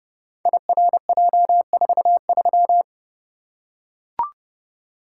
Morse code, heard as a keyed tone, about 29 wpm, 700 Hz. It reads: IRJ43
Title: Code Groups - 5 Letter:Numbers - 30wpm